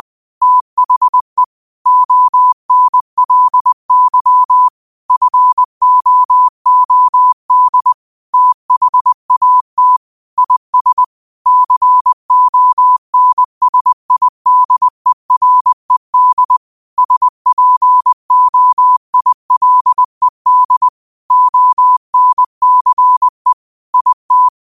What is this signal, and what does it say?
Morse code, heard as a keyed tone, about 20 wpm, 1000 Hz. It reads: THE ONLY FOOD THAT IS CONSIDERED SPOILED ONCE IT
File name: qotdfiles / 20wpm